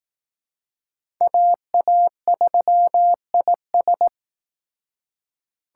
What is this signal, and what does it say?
Morse code, heard as a keyed tone, about 18 wpm, 700 Hz. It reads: AA3IS
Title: Speed Racing - USA 2x2 Call Signs 15wpm